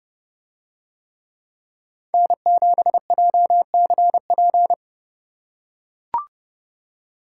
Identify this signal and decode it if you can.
Morse code, heard as a keyed tone, about 27 wpm, 700 Hz. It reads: N7JCP